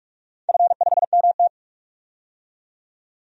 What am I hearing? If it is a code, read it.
Morse code, heard as a keyed tone, about 41 wpm, 700 Hz. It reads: F5MT
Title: DXCC-100 Call Signs - 4 Characters - 45wpm